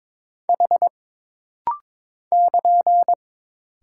Morse code, heard as a keyed tone, about 22 wpm, 700 Hz. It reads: H <KN>
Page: This rapid-fire timing is geared to push your ICR (Instant Character Recognition) skills.